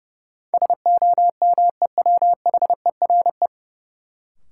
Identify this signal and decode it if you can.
Morse code, heard as a keyed tone, about 30 wpm, 700 Hz. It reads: SOMEWHERE
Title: Sentences from Top 1400 Words - New Word Drill Alt - 30wpm